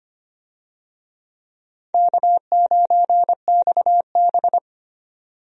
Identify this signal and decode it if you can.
Morse code, heard as a keyed tone, about 25 wpm, 700 Hz. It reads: K9XB
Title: USA Call Signs - 4 Characters - 25wpm